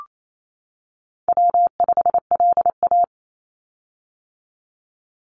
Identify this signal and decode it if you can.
Morse code, heard as a keyed tone, about 28 wpm, 700 Hz. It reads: W5LA
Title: Call Signs - 28wpm